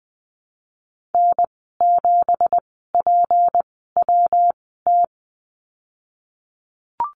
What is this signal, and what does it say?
Morse code, heard as a keyed tone, about 20 wpm, spaced as for 15 wpm, 700 Hz. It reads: N7PWT
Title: 193 Lesson 31g - Letter v - Call Signs 20wpm